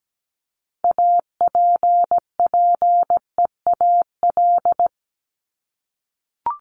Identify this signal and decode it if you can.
Morse code, heard as a keyed tone, about 17 wpm, 700 Hz. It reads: APPEAL
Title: Top 1251-1500 Words 17wpm